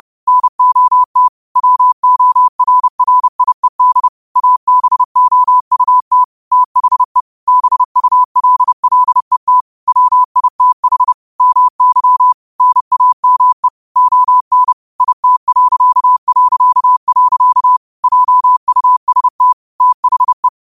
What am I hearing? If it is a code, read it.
Morse code, heard as a keyed tone, about 30 wpm, 1000 Hz. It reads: NOT WORRIED ABOUT THE BULLET WITH MY NAME ON IT... JUST THE
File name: qotdfiles / 30wpm